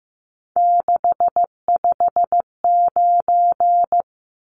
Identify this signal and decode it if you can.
Morse code, heard as a keyed tone, about 15 wpm, 700 Hz. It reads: 659
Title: Numbers - 3-Digit 15wpm